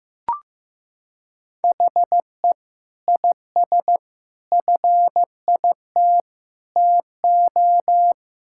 Morse code, heard as a keyed tone, about 16 wpm, 700 Hz. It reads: HE IS FIT TO
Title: Sentences from Top 1000 Words - Review Part 5 - 15wpm